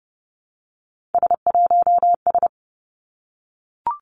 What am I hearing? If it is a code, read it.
Morse code, heard as a keyed tone, about 32 wpm, 700 Hz. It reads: S1S
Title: Three Challenging Letter-Number Combinations 30wpm